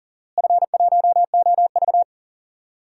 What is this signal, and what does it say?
Morse code, heard as a keyed tone, about 40 wpm, 700 Hz. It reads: F1OV
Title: DXCC-100 Call Signs - 4 Characters - Repeated 3x - 40wpm